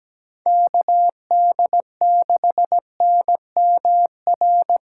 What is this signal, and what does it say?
Morse code, heard as a keyed tone, about 17 wpm, 700 Hz. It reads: KD6NMR
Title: USA Callsign DE Callsign 17wpm